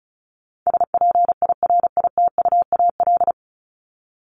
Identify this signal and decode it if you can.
Morse code, heard as a keyed tone, about 34 wpm, 700 Hz. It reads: SPIRITUAL
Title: Top 1501-1750 Words 35wpm